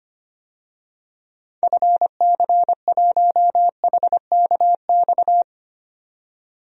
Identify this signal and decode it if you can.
Morse code, heard as a keyed tone, about 25 wpm, 700 Hz. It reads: FC1HKX
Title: DXCC-100 Call Signs - 6 Characters - 25wpm